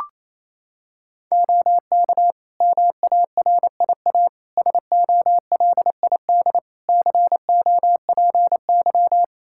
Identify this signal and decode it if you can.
Morse code, heard as a keyed tone, about 27 wpm, 700 Hz. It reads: OK MARIA SOLID COPY
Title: Practice QSO - Exchange 3 - More Info - 28wpm